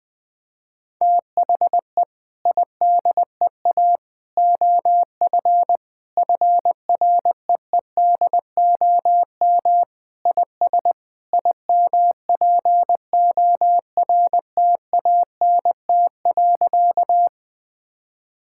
Morse code, heard as a keyed tone, about 20 wpm, 700 Hz. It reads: THE IDEA OF FREEDOM IS IMPORTANT.
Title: Sentences with 3-Word Phrases - 20wpm